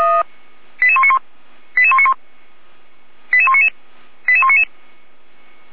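Five-tone selective call (ZVEI), heard as quick runs of five tones, tones of about 70 ms. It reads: 88191 88191 88199 88199
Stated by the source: ZVEI five-tone selective call received by SDR.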